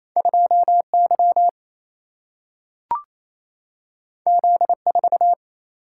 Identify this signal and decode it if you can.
Morse code, heard as a keyed tone, about 28 wpm, 700 Hz. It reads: 2 Y Z 4
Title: Single-Letter Single-Number 28wpm